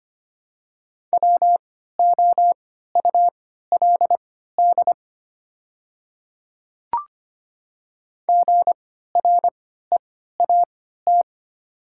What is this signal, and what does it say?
Morse code, heard as a keyed tone, about 25 wpm, spaced as for 14 wpm, 700 Hz. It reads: WOULD GREAT
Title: 3x Character Spacing - Top 300 Words - 25wpm